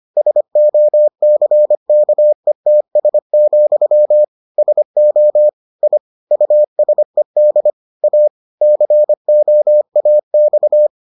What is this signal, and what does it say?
Morse code, heard as a keyed tone, about 25 wpm, 600 Hz. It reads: SOCKETS, SO I USED A COAX